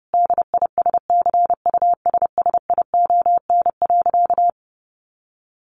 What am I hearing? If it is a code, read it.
Morse code, heard as a keyed tone, about 30 wpm, 700 Hz. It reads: DISCUSSION.
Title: Sentences from Top 1500 Words 30wpm